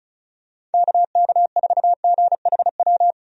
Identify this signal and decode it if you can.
Morse code, heard as a keyed tone, about 34 wpm, 700 Hz. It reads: KK4GHW
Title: USA Call Signs - 6 Characters - 35wpm